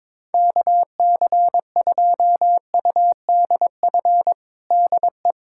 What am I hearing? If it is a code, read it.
Morse code, heard as a keyed tone, about 21 wpm, 700 Hz. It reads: KC2UDF DE
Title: USA Callsign DE Callsign - 6 Characters - 22wpm